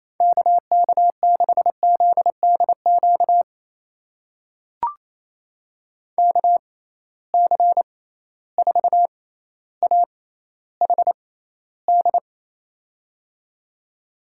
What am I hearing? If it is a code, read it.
Morse code, heard as a keyed tone, about 28 wpm, spaced as for 14 wpm, 700 Hz. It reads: KK6ZDQ KC4AHD
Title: USA Call Signs - 6 Characters - 6x Character Spacing - 28wpm